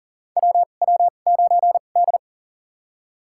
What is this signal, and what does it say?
Morse code, heard as a keyed tone, about 40 wpm, spaced as for 29 wpm, 700 Hz. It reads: WW9D
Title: USA Call Signs - 4 Characters - 2x Character Spacing - 40wpm